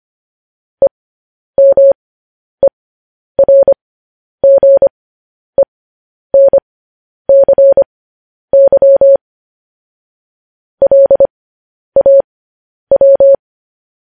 Morse code, heard as a keyed tone, about 25 wpm, spaced as for 10 wpm, 550 Hz. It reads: EMERGENCY LAW
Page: news headlines in morse code at 10 wpm